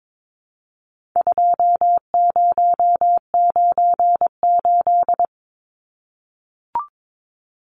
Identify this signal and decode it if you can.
Morse code, heard as a keyed tone, about 22 wpm, 700 Hz. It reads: 2098
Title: Numbers - 4-Digit 22wpm